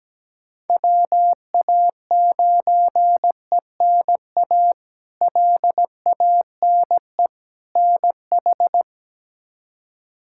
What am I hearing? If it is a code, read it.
Morse code, heard as a keyed tone, about 17 wpm, 700 Hz. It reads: WA9ENA LANE NH
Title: K1USN SST QSO Practice - 17wpm